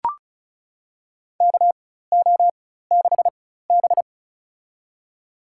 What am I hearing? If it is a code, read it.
Morse code, heard as a keyed tone, about 34 wpm, spaced as for 16 wpm, 700 Hz. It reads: KO6B
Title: USA Call Signs - 4 Characters - 4x Character Spacing - 35wpm